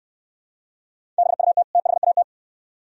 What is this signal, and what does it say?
Morse code, heard as a keyed tone, about 69 wpm, 700 Hz. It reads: BUT THAT